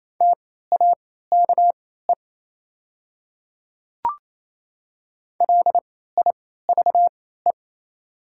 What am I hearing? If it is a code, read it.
Morse code, heard as a keyed tone, about 28 wpm, spaced as for 16 wpm, 700 Hz. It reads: TAKE LIVE
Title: 3x Character Spacing - Top 300 Words - 28wpm